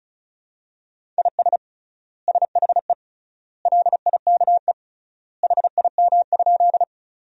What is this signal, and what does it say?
Morse code, heard as a keyed tone, about 36 wpm, 700 Hz. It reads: IS SHE LIKE HIM?
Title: 3x Word-Spacing - Sentences from Top 100 Words - 35wpm